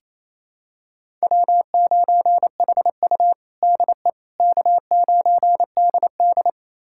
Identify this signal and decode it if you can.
Morse code, heard as a keyed tone, about 28 wpm, 700 Hz. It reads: W9HU DE K9DD
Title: USA Callsign DE Callsign - 4 Characters - 28wpm